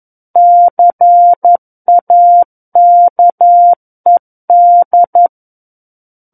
Morse code, heard as a keyed tone, about 11 wpm, 700 Hz. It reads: CAKED
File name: 11 WPM